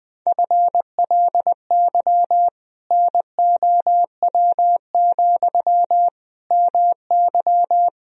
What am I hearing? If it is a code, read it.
Morse code, heard as a keyed tone, about 20 wpm, 700 Hz. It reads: FLY NOW, MY
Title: Sentences from Top 1000 Words - Review Part 2 - 20wpm